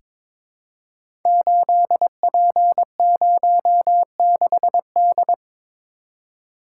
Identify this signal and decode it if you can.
Morse code, heard as a keyed tone, about 22 wpm, 700 Hz. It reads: 8P06D
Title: Code Groups - 5 Letter:Numbers - 22wpm